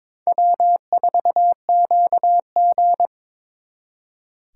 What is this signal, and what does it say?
Morse code, heard as a keyed tone, about 26 wpm, 700 Hz. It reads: W4QG
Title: Speed Racing - USA 1x2 Call Signs 22wpm